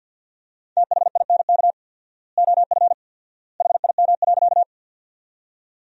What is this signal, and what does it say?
Morse code, heard as a keyed tone, about 50 wpm, 700 Hz. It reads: THINK OF HIM.
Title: 4x Word-Spacing - Sentences from Top 200 Words - 50wpm